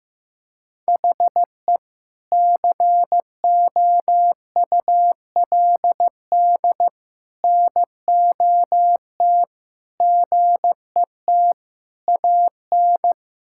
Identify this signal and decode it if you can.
Morse code, heard as a keyed tone, about 14 wpm, 700 Hz. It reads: HE COULD NOT GET AN
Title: Sentences from Top 1000 Words - Review Part 5 - 15wpm